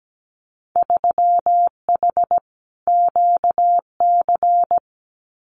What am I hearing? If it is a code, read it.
Morse code, heard as a keyed tone, about 17 wpm, 690 Hz. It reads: 3H QC
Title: Winter Field Day Exchanges 17wpm